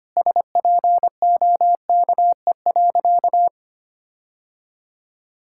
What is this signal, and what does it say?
Morse code, heard as a keyed tone, about 25 wpm, 700 Hz. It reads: SPOKE.
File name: practice / 25wpm / Sentences with 3-Word Phrases - 25wpm